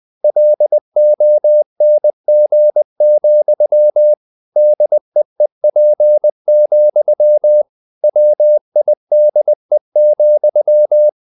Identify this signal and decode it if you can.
Morse code, heard as a keyed tone, about 20 wpm, 600 Hz. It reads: LONG, DEEP, WIDE,
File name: the old man and the sea part 2 20wpm